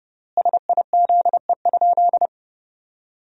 Morse code, heard as a keyed tone, about 32 wpm, 700 Hz. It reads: SIZE?
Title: 4x Word-Spacing - Sentences from Top 500 Words - 30wpm